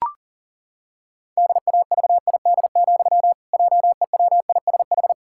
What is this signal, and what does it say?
Morse code, heard as a keyed tone, about 37 wpm, 700 Hz. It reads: DAVID, JEWISH
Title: Pangram Challenge v2 - 40wpm